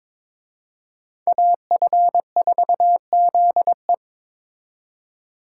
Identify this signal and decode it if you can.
Morse code, heard as a keyed tone, about 26 wpm, 700 Hz. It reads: AF4ZE
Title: Speed Racing - USA 2x2 Call Signs 22wpm